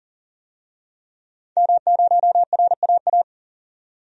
Morse code, heard as a keyed tone, about 40 wpm, 700 Hz. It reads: M0RAA
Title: DXCC-100 Call Signs - 5 Characters - 40wpm